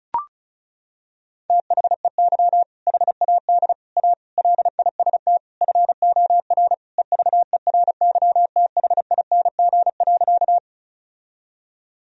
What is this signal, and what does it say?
Morse code, heard as a keyed tone, about 34 wpm, 700 Hz. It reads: THEY HAD A LIST FOR EVERYTHING.
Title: Sentences from Top 1000 Words - Review Part 2 - 35wpm